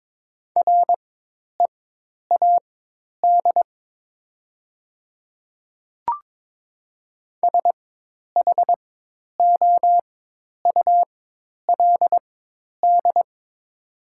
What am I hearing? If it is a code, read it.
Morse code, heard as a keyed tone, about 22 wpm, spaced as for 10 wpm, 700 Hz. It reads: READ SHOULD
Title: 4x Character Spacing - Top 500 Words - 22wpm